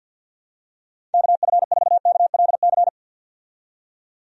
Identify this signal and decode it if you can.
Morse code, heard as a keyed tone, about 51 wpm, 700 Hz. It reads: KF4KLC
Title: USA Call Signs - 6 Characters - 50wpm